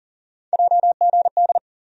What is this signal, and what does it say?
Morse code, heard as a keyed tone, about 39 wpm, 700 Hz. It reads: JGD
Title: USA Call Sign Postfixes - 3 Letters 40wpm